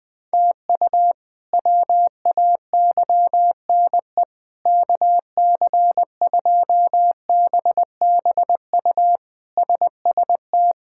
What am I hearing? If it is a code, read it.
Morse code, heard as a keyed tone, about 20 wpm, 700 Hz. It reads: TU WAYNE KC2BBU SST
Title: K1USN SST QSO Practice - 20wpm